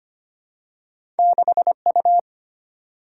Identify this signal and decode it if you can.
Morse code, heard as a keyed tone, about 25 wpm, 700 Hz. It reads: 6U
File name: Two Letter-Number Combinations - 25wpm